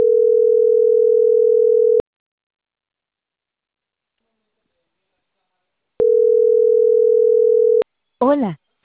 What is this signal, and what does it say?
Ringback tone (North America), 440 + 480 Hz, with speech.
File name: CENTRAL CAEA / out agcaea01 - .73094